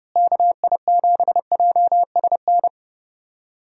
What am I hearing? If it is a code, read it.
Morse code, heard as a keyed tone, about 30 wpm, 700 Hz. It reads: KI7JSN
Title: USA Call Signs - 6 Characters - Repeated 3x - 30wpm